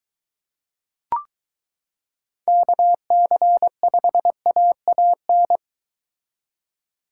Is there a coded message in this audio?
Yes: Morse code from 2.5 to 5.6 s, heard as a keyed tone, about 22 wpm, 700 Hz. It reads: KC5AAN